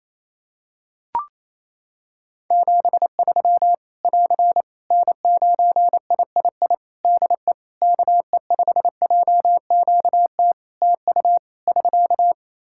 Morse code, heard as a keyed tone, about 28 wpm, 700 Hz. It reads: 73 <AR> N9III DE KE5JQT TU <SK>
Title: Practice QSO - Exchange 4 - Ending - 28wpm